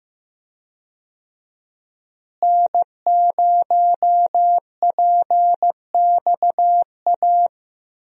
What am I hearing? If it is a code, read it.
Morse code, heard as a keyed tone, about 15 wpm, 700 Hz. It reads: N0PXA
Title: USA Call Signs - 5 Characters - Repeated 3x - 15wpm